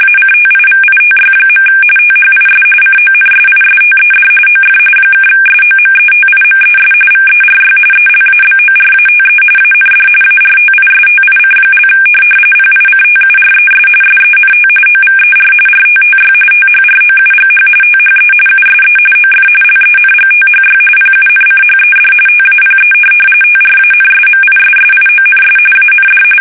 RTTY, about 45.45 baud. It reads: ACTIVITY. THIS CONSISTED OF (SOMETIMES VERY ELABORATE AND ARTISTIC) PICTURES SENT OVER RTTY THROUGH THE USE OF LENGTHY PUNCHED TAPE TRANSMISSIONS AND
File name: fsk850 / teletype-RTTY